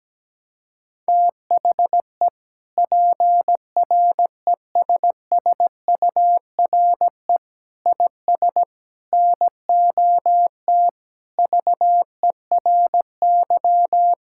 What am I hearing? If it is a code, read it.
Morse code, heard as a keyed tone, about 17 wpm, 700 Hz. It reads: THE PRESSURE IS NOT VERY
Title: Sentences from Top 1000 Words - Review Part 6 - 17wpm